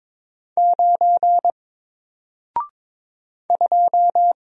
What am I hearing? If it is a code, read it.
Morse code, heard as a keyed tone, about 22 wpm, 700 Hz. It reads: 9 2
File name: Single-Letter-Number-22wpm